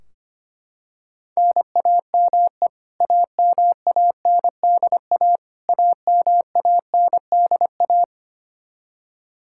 Morse code, heard as a keyed tone, about 25 wpm, 700 Hz. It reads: NAME AMANDA AMANDA
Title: K1USN SST QSO Practice 25wpm